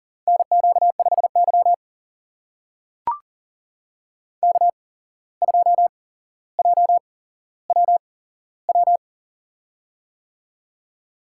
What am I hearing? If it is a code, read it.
Morse code, heard as a keyed tone, about 40 wpm, spaced as for 16 wpm, 700 Hz. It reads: NQ5Y K2JWW
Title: USA Call Signs - 8x Character Spacing - 40wpm